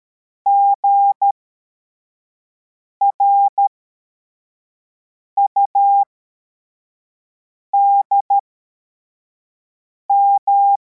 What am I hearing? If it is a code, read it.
Morse code, heard as a keyed tone, about 13 wpm, spaced as for 4 wpm, 800 Hz. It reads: GRUDM
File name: formation / cw / Text53